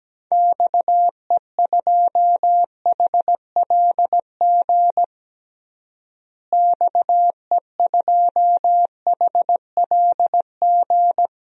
Morse code, heard as a keyed tone, about 17 wpm, 700 Hz. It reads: XE2HLG XE2HLG
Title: DXCC-100 Call Signs - 6 Characters - Repeated 2x - 17wpm